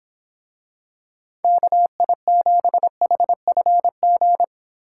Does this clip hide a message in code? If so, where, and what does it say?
Yes: Morse code from 1.4 to 4.4 s, heard as a keyed tone, about 25 wpm, 700 Hz. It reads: KI7HFG